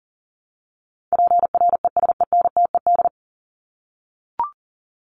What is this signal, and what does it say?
Morse code, heard as a keyed tone, about 40 wpm, 700 Hz. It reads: PRESENTED
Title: ED Suffix - Top 100 Words with Base Words 40wpm